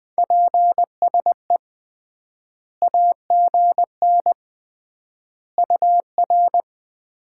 Morse code, heard as a keyed tone, about 20 wpm, 700 Hz. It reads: PSE AGN UR
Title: Common QSO Phrases - 3x Word-Spacing - 20wpm